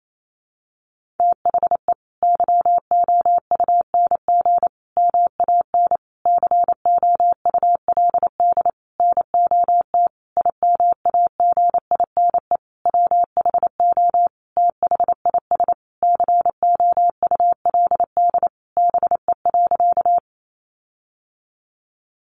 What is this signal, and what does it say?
Morse code, heard as a keyed tone, about 28 wpm, 700 Hz. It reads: THE YOUNG MAN COULD NOT IMAGINE WHO THIS COULD BE.
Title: Sentences from Top 1400 Words 28wpm